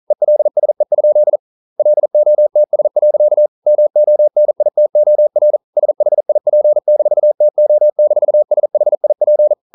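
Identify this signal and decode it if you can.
Morse code, heard as a keyed tone, about 42 wpm, 600 Hz. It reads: ELSE? LOTS. MONITOR SHIP-TO-SHIP
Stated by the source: Morse code practice at 41 wpm, from CW med Gnister.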